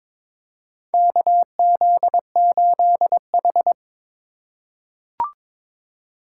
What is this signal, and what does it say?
Morse code, heard as a keyed tone, about 22 wpm, 700 Hz. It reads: KZ8H
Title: USA Call Signs - 4 Characters - 3x Character Spacing - 22wpm